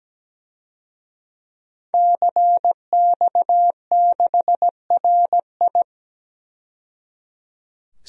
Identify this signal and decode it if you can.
Morse code, heard as a keyed tone, about 17 wpm, 700 Hz. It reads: CX6RI
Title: DXCC-100 Call Signs - 5 Characters - 17wpm